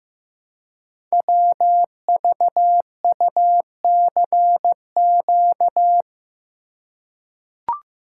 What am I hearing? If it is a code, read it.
Morse code, heard as a keyed tone, about 15 wpm, 700 Hz. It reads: WVUCQ
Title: Code Groups - 5 Letters - 15wpm